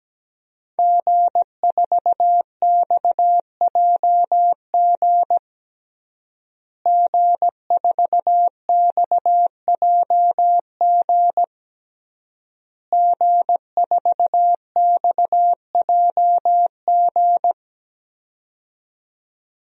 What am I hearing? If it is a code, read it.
Morse code, heard as a keyed tone, about 17 wpm, 700 Hz. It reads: G4XJG G4XJG G4XJG
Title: DXCC-100 Call Signs - 5 Characters - Repeated 3x - 17wpm